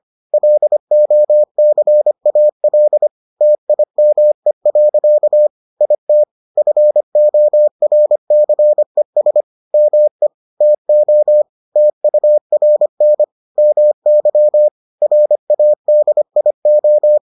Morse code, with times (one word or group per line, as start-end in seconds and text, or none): 0.3-3.1 LOCAL
3.4-5.5 TIME.
5.8-6.2 IT
6.6-9.4 FORCES
9.7-10.3 ME
10.6-11.4 TO
11.8-13.2 TURN
13.6-14.7 MY
15.0-17.2 RADIO